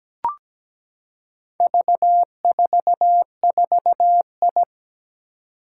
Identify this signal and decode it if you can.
Morse code, heard as a keyed tone, about 17 wpm, 700 Hz. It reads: V44I
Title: Four Challenging Letter-Number Combinations 17wpm